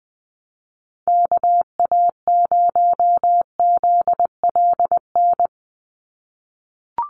Morse code, heard as a keyed tone, about 20 wpm, 700 Hz. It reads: KA0ZLN
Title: USA Call Signs 20wpm